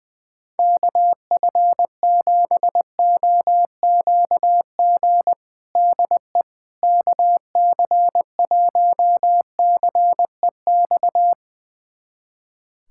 Morse code, heard as a keyed tone, about 21 wpm, 700 Hz. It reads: KF7OQG DE KC1CEX
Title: USA Callsign DE Callsign - 6 Characters - 20wpm